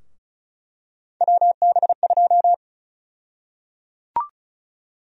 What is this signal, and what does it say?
Morse code, heard as a keyed tone, about 36 wpm, 700 Hz. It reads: WB2
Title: USA Call Sign Prefixes - Repeated 2x - 35wpm